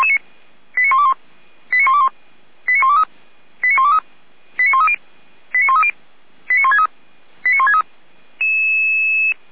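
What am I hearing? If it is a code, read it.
Five-tone selective call (ZVEI), heard as quick runs of five tones, tones of about 70 ms. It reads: 89121 89121 89123 89123 89130 89130 89163 89163
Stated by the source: ZVEI five-tone selective call received with an SDR, recorded by feuerwehr-orlamuende fire brigade.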